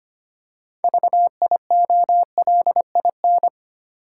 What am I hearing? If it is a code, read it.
Morse code, heard as a keyed tone, about 25 wpm, 700 Hz. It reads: VIOLIN
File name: practice / 25wpm / Easy Words with Hints - 25wpm